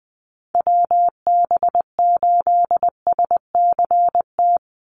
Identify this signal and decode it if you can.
Morse code, heard as a keyed tone, about 20 wpm, 700 Hz. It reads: WB8SCT
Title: Call Signs - 20wpm